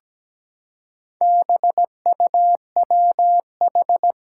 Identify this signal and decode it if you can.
Morse code, heard as a keyed tone, about 17 wpm, 700 Hz. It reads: BUWH
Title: Four Challenging Letter-Number Combinations 17wpm